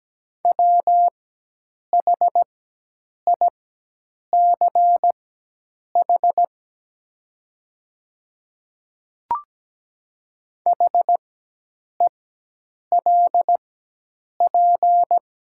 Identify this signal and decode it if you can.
Morse code, heard as a keyed tone, about 17 wpm, spaced as for 8 wpm, 700 Hz. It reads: WHICH HELP
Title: 4x Character Spacing - Top 300 Words - 17wpm